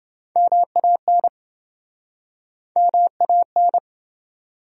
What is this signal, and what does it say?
Morse code, heard as a keyed tone, about 28 wpm, 700 Hz. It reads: MAN MAN